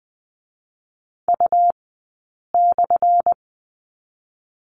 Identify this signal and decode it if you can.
Morse code, heard as a keyed tone, about 20 wpm, 700 Hz. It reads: U /
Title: 217 Lesson 35d - Number 0 - Two Characters 20wpm